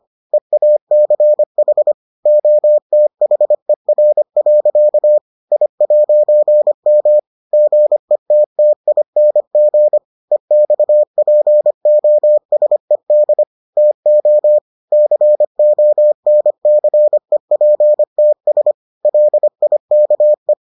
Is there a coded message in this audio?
Yes: Morse code from 0.3 to 20.5 s, heard as a keyed tone, about 25 wpm, 600 Hz. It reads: EACH OTHER. I'M GETTING EXPOSED TO CONCEPTS LIKE